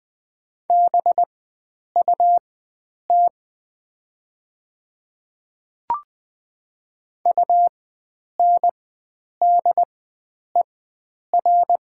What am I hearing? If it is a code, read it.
Morse code, heard as a keyed tone, about 20 wpm, spaced as for 9 wpm, 700 Hz. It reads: BUT UNDER
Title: 4x Character Spacing - Top 500 Words - 20wpm